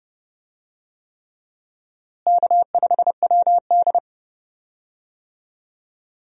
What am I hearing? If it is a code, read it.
Morse code, heard as a keyed tone, about 32 wpm, 700 Hz. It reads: K5WD
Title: USA Call Signs - 4 Characters - Repeated 3x - 30wpm